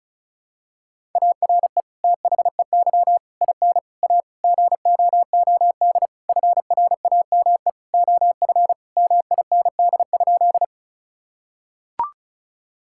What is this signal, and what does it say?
Morse code, heard as a keyed tone, about 36 wpm, 700 Hz. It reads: ARE THEY IN A GOOD FRAME OF MIND?
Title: Sentences from Top 1000 Words - Review Part 6 - 35wpm